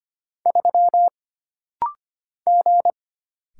Morse code, heard as a keyed tone, about 25 wpm, 700 Hz. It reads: 3 G
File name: Single Character (Letter:Number:Punctuation:Prosign) - Rapid-Fire - 25wpm